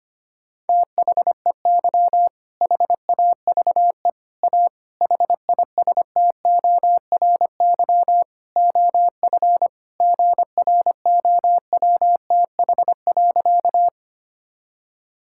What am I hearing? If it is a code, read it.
Morse code, heard as a keyed tone, about 25 wpm, 700 Hz. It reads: THEY HAVE A HISTORY OF GROWTH.
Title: Sentences from Top 1000 Words - Review Part 5 - 25wpm